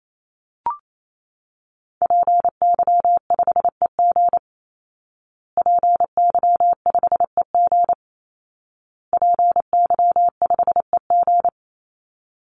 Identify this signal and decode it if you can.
Morse code, heard as a keyed tone, about 28 wpm, 700 Hz. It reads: PY5EG PY5EG PY5EG
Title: Call Signs Repeated 2x - 28wpm